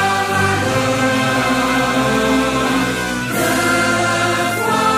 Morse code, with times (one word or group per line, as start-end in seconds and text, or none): none